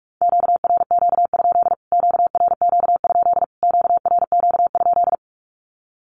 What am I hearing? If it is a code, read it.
Morse code, heard as a keyed tone, about 44 wpm, 700 Hz. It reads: QRQ? QRQ? QRQ?
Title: Q-Codes Repeated 3x 45wpm